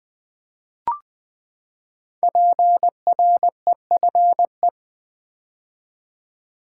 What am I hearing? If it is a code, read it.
Morse code, heard as a keyed tone, about 20 wpm, 700 Hz. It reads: PREFE
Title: Build A Word - 5 Letters At A Time - 10-Letter Words - 20wpm